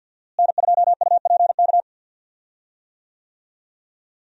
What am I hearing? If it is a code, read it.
Morse code, heard as a keyed tone, about 46 wpm, 700 Hz. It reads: N2UPK
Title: USA Call Signs - 5 Characters - Repeated 3x - 50wpm